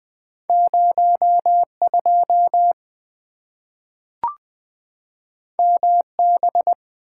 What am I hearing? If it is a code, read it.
Morse code, heard as a keyed tone, about 20 wpm, 700 Hz. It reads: 0 2 M B
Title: Two Letter-Number Combinations - 20wpm